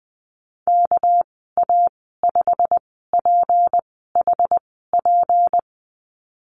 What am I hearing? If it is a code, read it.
Morse code, heard as a keyed tone, about 20 wpm, spaced as for 14 wpm, 700 Hz. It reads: KA5PHP
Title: 200 Lesson 32g - Letter k - Call Signs 20wpm